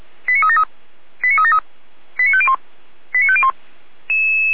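Five-tone selective call (ZVEI), heard as quick runs of five tones, tones of about 70 ms. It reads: 89373 89373 89501 89501
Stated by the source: ZVEI five-tone selective call received on an SDR.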